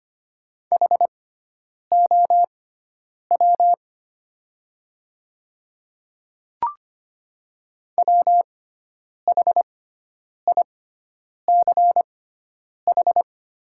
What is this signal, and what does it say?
Morse code, heard as a keyed tone, about 25 wpm, spaced as for 9 wpm, 690 Hz. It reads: HOW WHICH
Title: 6x Character Spacing - Top 100 Words - 25wpm